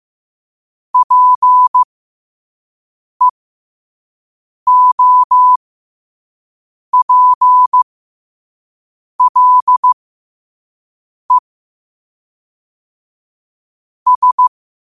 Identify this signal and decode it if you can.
Morse code, heard as a keyed tone, about 15 wpm, spaced as for 5 wpm, 1000 Hz. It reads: PEOPLE S
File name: qotdfiles / 5wpm